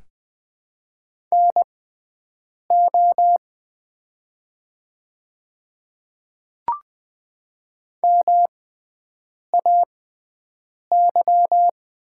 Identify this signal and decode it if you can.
Morse code, heard as a keyed tone, about 20 wpm, spaced as for 7 wpm, 700 Hz. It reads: NO MAY
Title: 6x Character Spacing - Top 100 Words - 20wpm